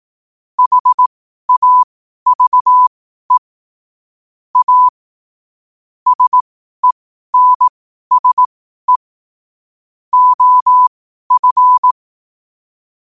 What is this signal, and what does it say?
Morse code, heard as a keyed tone, about 18 wpm, spaced as for 12 wpm, 1000 Hz. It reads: HAVE A SENSE OF
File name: qotdfiles / 12wpm